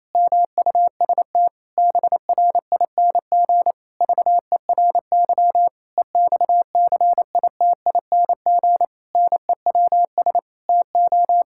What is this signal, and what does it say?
Morse code, heard as a keyed tone, about 28 wpm, 700 Hz. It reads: MUST BRING VERY EXCITING NEWS TO
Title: Pangram Challenge v2 - 28wpm